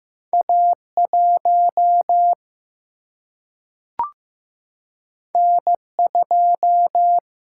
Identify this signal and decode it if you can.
Morse code, heard as a keyed tone, about 15 wpm, 700 Hz. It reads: A 1 N 2
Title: Single-Letter Single-Number 15wpm